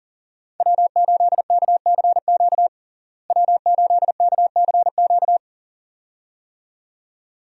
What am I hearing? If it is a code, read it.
Morse code, heard as a keyed tone, about 40 wpm, 700 Hz. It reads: W8KCQ W8KCQ
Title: USA Call Signs - 5 Characters - Repeated 2x - 40wpm